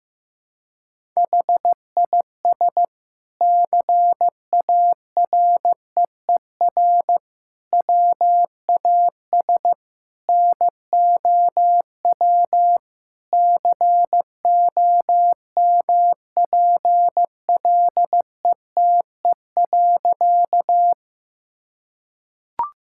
Morse code, heard as a keyed tone, about 15 wpm, 700 Hz. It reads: HIS CAREER WAS NOW COMPLETE.
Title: Sentences from Top 1000 Words - Review Part 6 - 15wpm